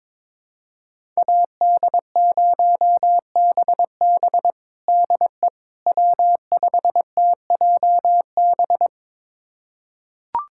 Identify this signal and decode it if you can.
Morse code, heard as a keyed tone, about 22 wpm, 700 Hz. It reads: AD0BB DE W5TJB
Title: USA Callsign DE Callsign - 5 Characters - 22wpm